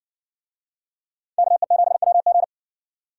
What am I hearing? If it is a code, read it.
Morse code, heard as a keyed tone, about 55 wpm, 700 Hz. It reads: KE7WG